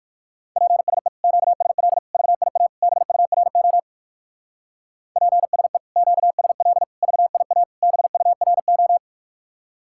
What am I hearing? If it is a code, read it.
Morse code, heard as a keyed tone, about 56 wpm, 700 Hz. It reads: PSE QSL VIA BURO PSE QSL VIA BURO
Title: Speed-Racing Common QSO Phrases - 45wpm